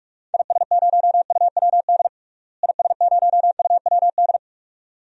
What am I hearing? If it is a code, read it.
Morse code, heard as a keyed tone, about 46 wpm, 700 Hz. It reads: IS0UWD IS0UWD
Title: DXCC-100 Call Signs - 6 Characters - Repeated 2x - 45wpm